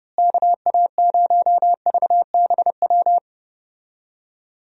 Morse code, heard as a keyed tone, about 31 wpm, 700 Hz. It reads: KA0VBW
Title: USA Call Signs - 6 Characters - Repeated 2x - 30wpm